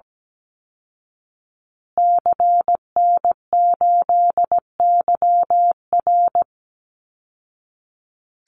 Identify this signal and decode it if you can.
Morse code, heard as a keyed tone, about 17 wpm, 700 Hz. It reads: CN8YR
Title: Call Signs Repeated 3x - 17wpm